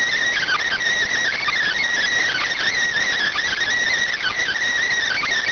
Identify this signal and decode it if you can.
Emergency Alert System (SAME) tones (header bursts), heard as harsh buzzing bursts. It is heard from 0.3 to 4.6 s.